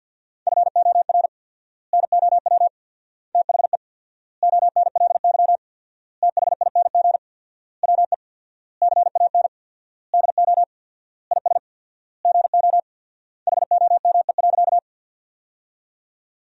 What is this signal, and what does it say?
Morse code, heard as a keyed tone, about 50 wpm, 700 Hz. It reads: FOR NOW THE ONLY THING WE CAN DO IS GO HOME.
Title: 4x Word-Spacing - Sentences from Top 200 Words - 50wpm